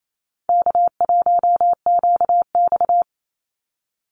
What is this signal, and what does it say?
Morse code, heard as a keyed tone, about 28 wpm, 700 Hz. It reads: K1QX
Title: Call Signs Repeated 2x - 28wpm